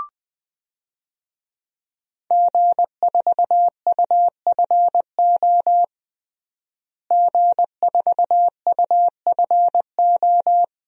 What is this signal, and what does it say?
Morse code, heard as a keyed tone, about 20 wpm, 700 Hz. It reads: G4UFO G4UFO
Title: DXCC-100 Call Signs - 5 Characters - Repeated 2x - 20wpm